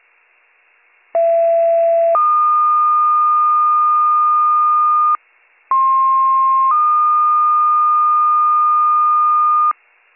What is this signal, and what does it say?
Two-tone page, heard as a one-second tone, then a three-second tone, about 670 then 1220 Hz, about 1060 then 1220 Hz. It is heard from 1.1 to 9.7 s.